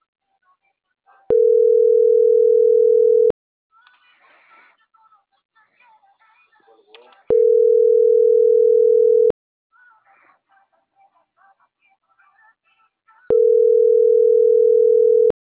Ringback tone (North America), 440 + 480 Hz.